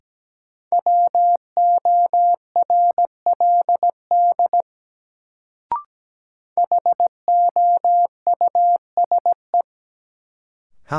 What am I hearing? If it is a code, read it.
Morse code, heard as a keyed tone, about 17 wpm, 700 Hz. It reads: WORLD HOUSE
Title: Sentences from Top 200 Words - New Word Drill - 17wpm